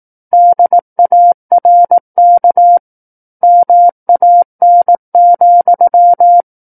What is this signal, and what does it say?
Morse code, heard as a keyed tone, about 18 wpm, 700 Hz. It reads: DARK MAN,
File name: 18 WPM